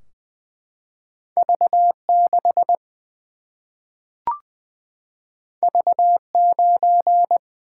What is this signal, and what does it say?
Morse code, heard as a keyed tone, about 20 wpm, 700 Hz. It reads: V 6 V 9
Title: Single-Letter Single-Number 20wpm